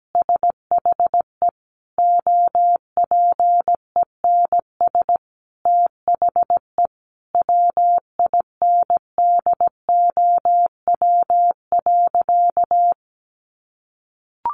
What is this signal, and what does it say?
Morse code, heard as a keyed tone, about 17 wpm, 700 Hz. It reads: SHE OPENS THE WINDOW.
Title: Easy Phrases 17wpm